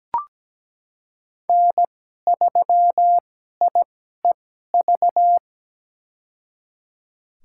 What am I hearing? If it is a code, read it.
Morse code, heard as a keyed tone, about 17 wpm, spaced as for 12 wpm, 700 Hz. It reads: N3IEV
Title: USA Call Signs - 5 Characters - 2x Character Spacing - 17wpm